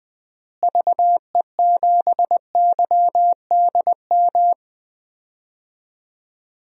Morse code, heard as a keyed tone, about 20 wpm, 700 Hz. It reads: VE7YDM
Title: DXCC-100 Call Signs - 6 Characters - Repeated 2x - 20wpm